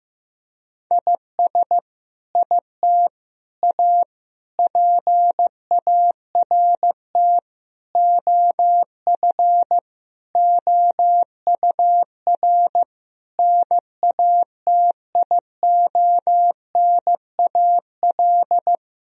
Morse code, heard as a keyed tone, about 15 wpm, 700 Hz. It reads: IS IT A PART OF OUR NATIONAL
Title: Sentences from Top 1000 Words - Review Part 4 - 15wpm